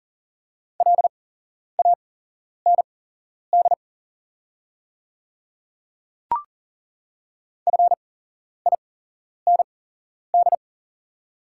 Morse code, heard as a keyed tone, about 40 wpm, spaced as for 11 wpm, 700 Hz. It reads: LAND FIND
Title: 8x Character Spacing - Top 200 Words - 40wpm